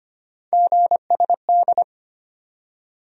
Morse code, heard as a keyed tone, about 25 wpm, 700 Hz. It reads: GSD
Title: USA Call Sign Postfixes - 3 Letters 25wpm